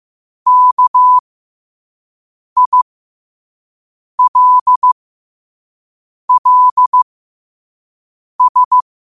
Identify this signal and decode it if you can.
Morse code, heard as a keyed tone, about 15 wpm, spaced as for 5 wpm, 1000 Hz. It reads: KILLS
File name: 5wpm